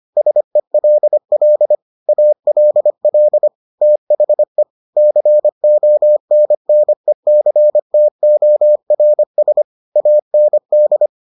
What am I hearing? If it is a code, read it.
Morse code, heard as a keyed tone, about 25 wpm, 600 Hz. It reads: SELL ALL THE CONNECTORS AND